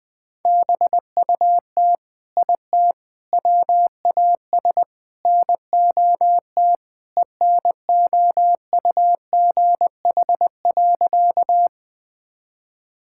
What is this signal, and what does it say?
Morse code, heard as a keyed tone, about 20 wpm, 700 Hz. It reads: BUT IT WAS NOT ENOUGH.
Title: Sentences with 4-Word Phrases - 20wpm